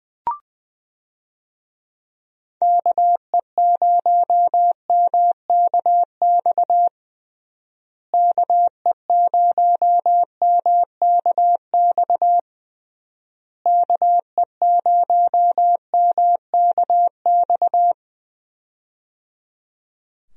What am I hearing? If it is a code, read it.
Morse code, heard as a keyed tone, about 21 wpm, 700 Hz. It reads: KE0MKX KE0MKX KE0MKX
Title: USA Call Signs - 6 Characters - Repeated 3x - 20wpm